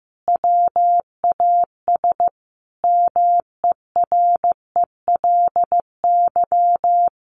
Morse code, heard as a keyed tone, about 15 wpm, 700 Hz. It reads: WAS MERELY
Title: Sentences from Top 1400 Words 15wpm